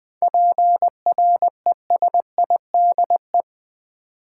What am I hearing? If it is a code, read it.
Morse code, heard as a keyed tone, about 20 wpm, 700 Hz. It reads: PRESIDE
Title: 7 Letter Words - First 3 Letters Given - 20wpm